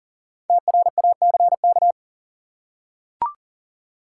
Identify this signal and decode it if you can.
Morse code, heard as a keyed tone, about 40 wpm, 700 Hz. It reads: TRACK
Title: Five Letter Words - 40wpm